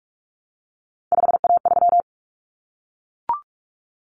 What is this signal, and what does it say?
Morse code, heard as a keyed tone, about 44 wpm, 700 Hz. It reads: 5A3
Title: Common Cut Numbers - 3-Digit 45wpm